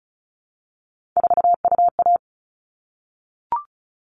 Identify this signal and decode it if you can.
Morse code, heard as a keyed tone, about 34 wpm, 700 Hz. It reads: 4UA
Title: All Cut Numbers - 3-Digit 35wpm